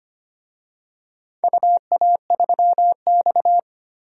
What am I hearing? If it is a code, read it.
Morse code, heard as a keyed tone, about 25 wpm, 690 Hz. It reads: UA3X
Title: DXCC-100 Call Signs - 4 Characters - 25wpm